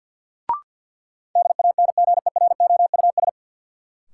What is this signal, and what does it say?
Morse code, heard as a keyed tone, about 47 wpm, 700 Hz. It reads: DANGEROUS
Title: Sentences from Top 1200 Words - New Word Drill Alt - 50wpm